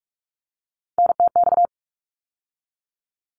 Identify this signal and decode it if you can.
Morse code, heard as a keyed tone, about 44 wpm, 700 Hz. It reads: NTX
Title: ARRL Sections 45wpm